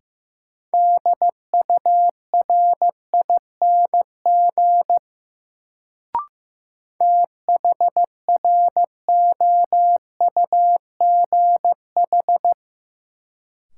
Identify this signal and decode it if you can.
Morse code, heard as a keyed tone, about 15 wpm, 700 Hz. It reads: DURING THROUGH
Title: Sets of 2 Words - New Word Drill - 15wpm